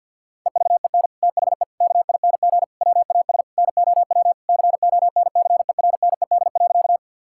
Morse code, heard as a keyed tone, about 50 wpm, 700 Hz. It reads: EVEN THE KING WAS NOW CONCERNED.
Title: Sentences from Top 1000 Words - Review Part 6 - 50wpm